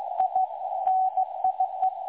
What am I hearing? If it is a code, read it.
Morse code, heard as a keyed tone, about 17 wpm, 750 Hz. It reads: I NIE
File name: dlazaawansowanych / nagraniazpasma / websdr recording 3565.0kHz